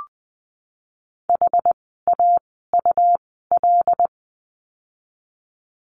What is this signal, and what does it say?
Morse code, heard as a keyed tone, about 20 wpm, spaced as for 14 wpm, 700 Hz. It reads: HAUL
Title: 101 Lesson 17f - Letter u - Words 20wpm